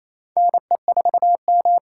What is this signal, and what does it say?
Morse code, heard as a keyed tone, about 28 wpm, 700 Hz. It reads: NE4M
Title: K1USN SST QSO Practice - 28wpm